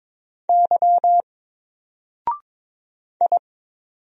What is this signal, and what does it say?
Morse code, heard as a keyed tone, about 22 wpm, 700 Hz. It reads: Y I
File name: Single-Letter-Number-22wpm